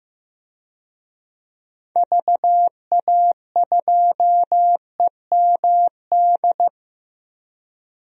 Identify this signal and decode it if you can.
Morse code, heard as a keyed tone, about 15 wpm, 700 Hz. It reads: VA2EMD
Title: DXCC-100 Call Signs - 6 Characters - Repeated 3x - 15wpm